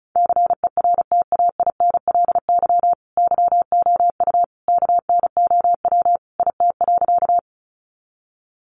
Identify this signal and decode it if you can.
Morse code, heard as a keyed tone, about 34 wpm, 700 Hz. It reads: CERTAINLY YOU KNOW IT.
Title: Sentences from Top 900 Words 35wpm